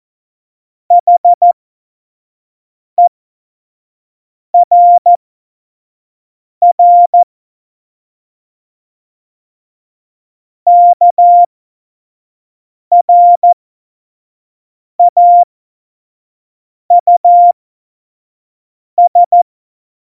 Morse code, heard as a keyed tone, about 14 wpm, spaced as for 5 wpm, 700 Hz. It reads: HERR KRAUS
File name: 25 F70